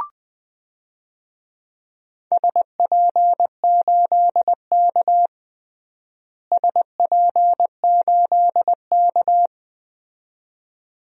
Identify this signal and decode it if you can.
Morse code, heard as a keyed tone, about 20 wpm, 700 Hz. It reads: SP8K SP8K
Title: DXCC-100 Call Signs - 4 Characters - Repeated 2x - 20wpm